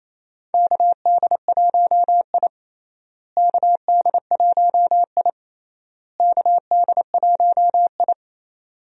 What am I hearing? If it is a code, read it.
Morse code, heard as a keyed tone, about 28 wpm, 700 Hz. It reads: KD1I KD1I KD1I
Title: USA Call Signs - 4 Characters - Repeated 3x - 28wpm